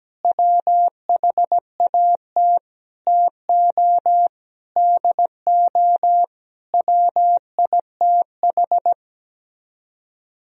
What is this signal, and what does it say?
Morse code, heard as a keyed tone, about 17 wpm, 700 Hz. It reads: WHAT TO DO WITH
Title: Speed-Racing - Sets of 5 Words 17wpm